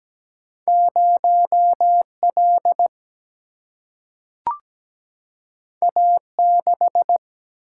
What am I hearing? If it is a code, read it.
Morse code, heard as a keyed tone, about 17 wpm, 700 Hz. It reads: 0 L A 6
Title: Single-Letter Single-Number 17wpm